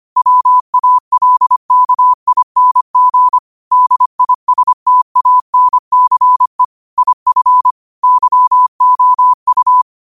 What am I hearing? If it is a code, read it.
Morse code, heard as a keyed tone, about 25 wpm, 1000 Hz. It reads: WALKING DISTANCE IF YOU